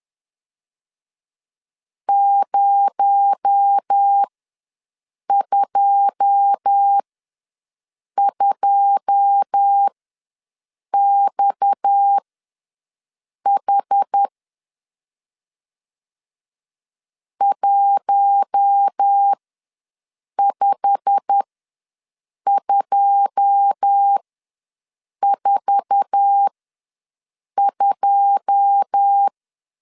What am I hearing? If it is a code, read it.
Morse code, heard as a keyed tone, about 10 wpm, spaced as for 6 wpm, 800 Hz. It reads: 022XH 15242
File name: LEZIONI / 9 2 ws 10wpm 4130